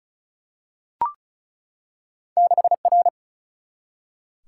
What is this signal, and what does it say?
Morse code, heard as a keyed tone, about 36 wpm, 700 Hz. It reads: 6R
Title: Single-Letter Single-Number 35wpm